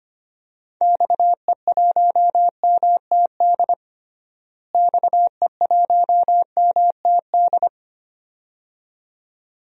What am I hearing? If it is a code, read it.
Morse code, heard as a keyed tone, about 25 wpm, 700 Hz. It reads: XE1MTD XE1MTD
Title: DXCC-100 Call Signs - 6 Characters - Repeated 2x - 25wpm